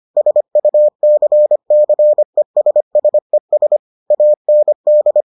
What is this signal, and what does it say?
Morse code, heard as a keyed tone, about 25 wpm, 600 Hz. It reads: SUCCESSES AND